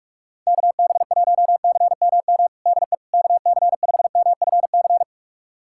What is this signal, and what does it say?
Morse code, heard as a keyed tone, about 46 wpm, 700 Hz. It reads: KB1CMM DE KC5MFC
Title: USA Callsign DE Callsign - 6 Characters - 45wpm